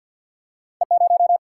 Morse code, heard as a keyed tone, about 48 wpm, 700 Hz. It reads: E 0
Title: Single-Letter Single-Number 50wpm